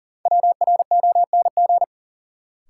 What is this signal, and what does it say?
Morse code, heard as a keyed tone, about 40 wpm, 700 Hz. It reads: WRONG
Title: Sentences from Top 900 Words - New Word Drill - 40wpm